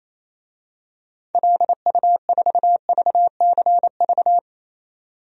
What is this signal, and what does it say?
Morse code, heard as a keyed tone, about 27 wpm, 700 Hz. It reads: LU4VCV
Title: DXCC-100 Call Signs - 6 Characters - 28wpm